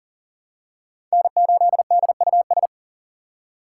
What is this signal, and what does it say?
Morse code, heard as a keyed tone, about 40 wpm, 700 Hz. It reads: N8DUS
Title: K1USN SST QSO Practice - 40wpm